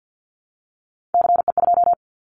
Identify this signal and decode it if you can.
Morse code, heard as a keyed tone, about 50 wpm, 700 Hz. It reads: CE2
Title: Callsign Prefixes for Top 100 DXCC - 50wpm